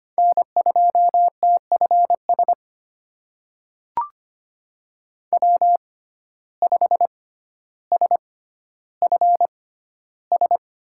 Morse code, heard as a keyed tone, about 25 wpm, spaced as for 13 wpm, 700 Hz. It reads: N2TFS W5SFS
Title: USA Call Signs - 5 Characters - 6x Character Spacing - 25wpm